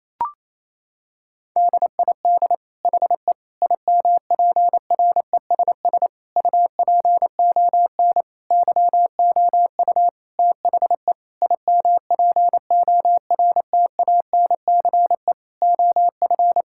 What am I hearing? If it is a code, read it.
Morse code, heard as a keyed tone, about 28 wpm, 700 Hz. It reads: DID HE IMPRESS UPON YOU THE IMPORTANCE OF
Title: Sentences from Top 1600 Words 28wpm